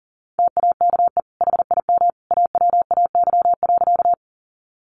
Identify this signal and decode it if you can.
Morse code, heard as a keyed tone, about 40 wpm, 700 Hz. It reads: TAKE HIM AWAY.
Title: Sentences from Top 600 Words 40wpm